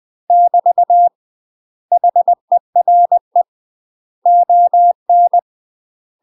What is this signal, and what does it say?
Morse code, heard as a keyed tone, about 20 wpm, 700 Hz. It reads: <BT> HERE ON
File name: UK MASTER QSOs TRAINING/20WPM